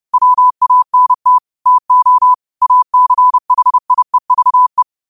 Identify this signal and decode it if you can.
Morse code, heard as a keyed tone, about 30 wpm, 1000 Hz. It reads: WANT TO ACHIEVE
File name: qotdfiles / 30wpm